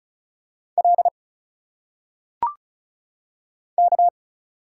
Morse code, heard as a keyed tone, about 36 wpm, 700 Hz. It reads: L K
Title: Single Letters - Phonetic Pronunciation 35wpm